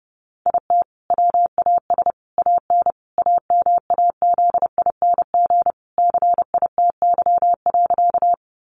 Morse code, heard as a keyed tone, about 30 wpm, 700 Hz. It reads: IT WAS AN AMAZING CITY.
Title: Sentences from Top 1500 Words 30wpm